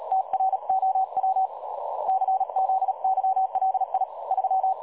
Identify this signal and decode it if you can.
Morse code, heard as a keyed tone, about 38 wpm, 760 Hz. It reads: TROW CZYLI 3
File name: dlazaawansowanych / nagraniazpasma / websdr recording 3565.0kHz